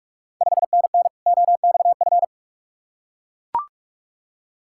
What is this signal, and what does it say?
Morse code, heard as a keyed tone, about 44 wpm, 690 Hz. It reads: 5NN OXF
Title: Ontario QSO Party Exchange 45wpm